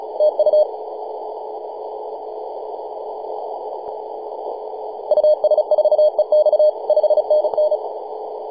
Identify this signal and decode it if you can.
Morse code, heard as a keyed tone, about 37 wpm, 610 Hz. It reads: TU US4EX 5NN
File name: s79hn-24-910mhz-cw